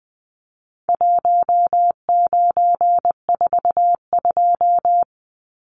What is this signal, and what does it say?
Morse code, heard as a keyed tone, about 19 wpm, 700 Hz. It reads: 1942
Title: Numbers - Contest Style - 4-Digit 20wpm